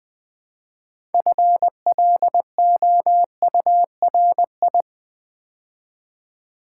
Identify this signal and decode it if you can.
Morse code, heard as a keyed tone, about 20 wpm, 700 Hz. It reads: FLOURI
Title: Build A Word - 2 Letters At A Time - 8-Letter Words - 20wpm